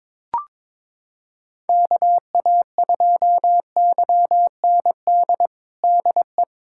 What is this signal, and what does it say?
Morse code, heard as a keyed tone, about 22 wpm, 700 Hz. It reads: KA2YND DE
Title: USA Callsign DE Callsign - 6 Characters - 22wpm